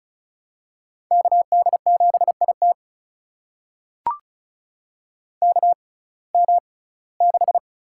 Morse code, heard as a keyed tone, about 36 wpm, spaced as for 21 wpm, 700 Hz. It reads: KD7IT KM6
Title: USA Call Signs - 6x Character Spacing - 35wpm